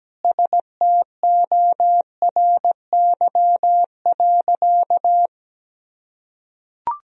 Morse code, heard as a keyed tone, about 17 wpm, 700 Hz. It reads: STORY.
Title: Sentences from Top 300 Words and Abbreviations 17wpm